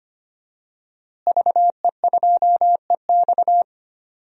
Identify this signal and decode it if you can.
Morse code, heard as a keyed tone, about 25 wpm, 700 Hz. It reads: VE2EX
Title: DXCC-100 Call Signs - 5 Characters - Repeated 2x - 25wpm